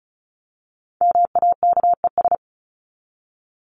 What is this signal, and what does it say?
Morse code, heard as a keyed tone, about 34 wpm, 700 Hz. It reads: MAKES
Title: ES:S Suffix - Top 100 Words with Base Words 35wpm